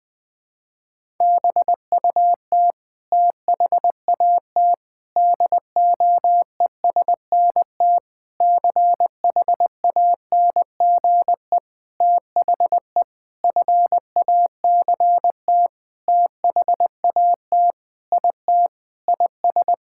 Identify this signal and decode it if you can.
Morse code, heard as a keyed tone, about 20 wpm, 700 Hz. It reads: BUT THAT DOESNT CHANGE THE FACT THAT IT IS
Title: Sentences with 7-Word Phrases - 20wpm